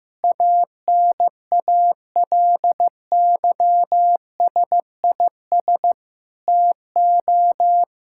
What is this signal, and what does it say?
Morse code, heard as a keyed tone, about 15 wpm, 700 Hz. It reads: ANALYSIS TO
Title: Sentences from Top 1000 Words - Review Part 5 - 15wpm